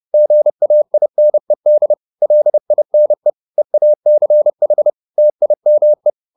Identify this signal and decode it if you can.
Morse code, heard as a keyed tone, about 30 wpm, 600 Hz. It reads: GAINED LINE EACH TIME
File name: the old man and the sea 2 30wpm